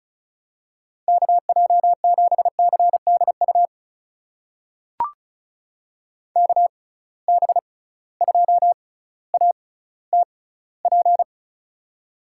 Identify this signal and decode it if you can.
Morse code, heard as a keyed tone, about 36 wpm, spaced as for 18 wpm, 700 Hz. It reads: KJ7CDU KB2ATP
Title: USA Call Signs - 6 Characters - 6x Character Spacing - 35wpm